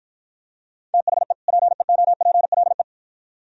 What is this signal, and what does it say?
Morse code, heard as a keyed tone, about 51 wpm, 700 Hz. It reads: THE PEOPLE